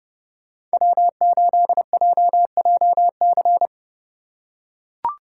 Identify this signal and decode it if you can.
Morse code, heard as a keyed tone, about 30 wpm, 700 Hz. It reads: W8JJC
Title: USA Call Signs - 5 Characters - 2x Character Spacing - 30wpm